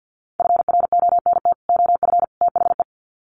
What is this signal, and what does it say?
Morse code, heard as a keyed tone, about 50 wpm, 700 Hz. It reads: FRONT OF THE
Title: Sets of 3 Words 50wpm